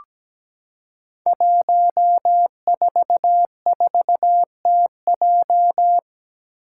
Morse code, heard as a keyed tone, about 17 wpm, 700 Hz. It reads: 144TJ
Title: Five Challenging Letter-Number Combinations 17wpm